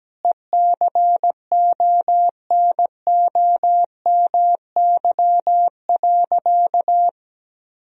Morse code, heard as a keyed tone, about 17 wpm, 700 Hz. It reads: ECONOMY.
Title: Sentences from Top 1000 Words - Review Part 3 - 17wpm